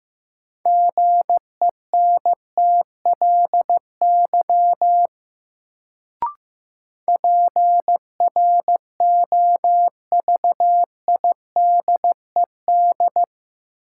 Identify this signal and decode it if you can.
Morse code, heard as a keyed tone, about 15 wpm, 700 Hz. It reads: GENTLY PROVIDED
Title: Sentences from Top 1500 Words - New Word Drill Alt - 15wpm